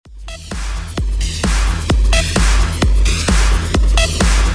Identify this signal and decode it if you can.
There is no code.